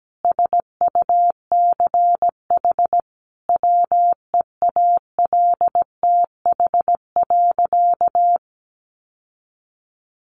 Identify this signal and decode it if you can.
Morse code, heard as a keyed tone, about 17 wpm, 700 Hz. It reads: SUCH WEALTH.
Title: Sentences from Top 1200 Words 17wpm